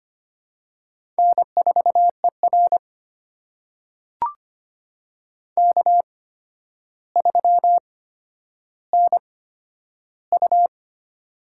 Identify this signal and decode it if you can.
Morse code, heard as a keyed tone, about 25 wpm, spaced as for 11 wpm, 700 Hz. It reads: N4ER K3NU
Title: USA Call Signs - 4 Characters - 8x Character Spacing - 25wpm